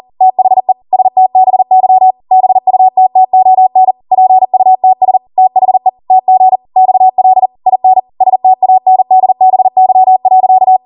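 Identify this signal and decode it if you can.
Morse code, heard as a keyed tone, about 40 wpm, 750 Hz. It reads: THE STBY BUTTON PUTS THE TG XL IN STANDBY.
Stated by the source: Code practice at 40 wpm.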